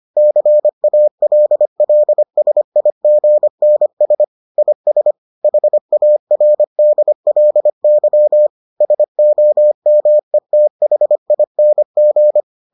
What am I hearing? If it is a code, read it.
Morse code, heard as a keyed tone, about 25 wpm, 600 Hz. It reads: CALLSIGNS IS HARDLY SOMETHING